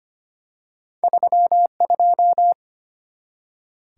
Morse code, heard as a keyed tone, about 25 wpm, 700 Hz. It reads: 32
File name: Two Letter-Number Combinations - 25wpm